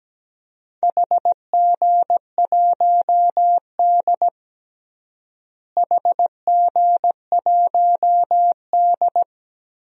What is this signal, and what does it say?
Morse code, heard as a keyed tone, about 17 wpm, 700 Hz. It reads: HG1D HG1D
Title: DXCC-100 Call Signs - 4 Characters - Repeated 2x - 17wpm